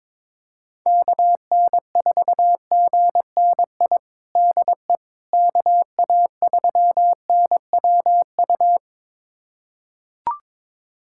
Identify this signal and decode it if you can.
Morse code, heard as a keyed tone, about 22 wpm, 700 Hz. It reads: KN4GNI DE KA3NWU
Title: USA Callsign DE Callsign - 6 Characters - 22wpm